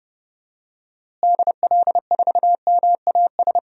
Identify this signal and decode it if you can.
Morse code, heard as a keyed tone, about 32 wpm, 700 Hz. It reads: DL4MAS
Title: DXCC-100 Call Signs - 6 Characters - Repeated 3x - 30wpm